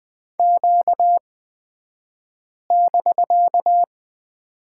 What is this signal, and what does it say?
Morse code, heard as a keyed tone, about 20 wpm, spaced as for 5 wpm, 700 Hz. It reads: Q<BK>
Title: 249 Lesson 40d - BK - Two Characters 20wpm